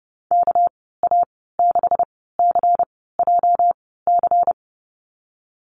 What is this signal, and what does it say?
Morse code, heard as a keyed tone, about 30 wpm, spaced as for 17 wpm, 700 Hz. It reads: KA6CJC
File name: Lesson 37g - Letter x - Call Signs 30wpm